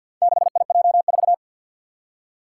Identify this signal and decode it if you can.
Morse code, heard as a keyed tone, about 51 wpm, 700 Hz. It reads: 6IJ4
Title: Four Challenging Letter-Number Combinations 50wpm